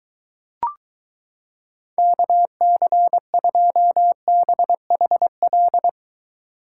Morse code, heard as a keyed tone, about 23 wpm, 700 Hz. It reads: KC2BHL